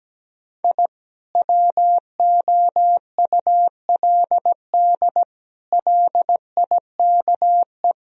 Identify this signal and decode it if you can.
Morse code, heard as a keyed tone, about 17 wpm, 700 Hz. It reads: I WOULD LIKE
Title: Sentences from Top 1000 Words - Review Part 4 - 17wpm